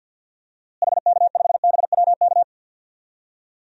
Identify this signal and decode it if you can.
Morse code, heard as a keyed tone, about 48 wpm, 700 Hz. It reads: HK5BWK
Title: DXCC-100 Call Signs - 6 Characters - Repeated 3x - 50wpm